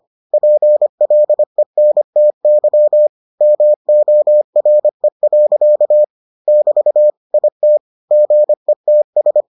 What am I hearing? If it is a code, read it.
Morse code, heard as a keyed tone, about 25 wpm, 600 Hz. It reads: PLENTY MORE. <BT> IT GETS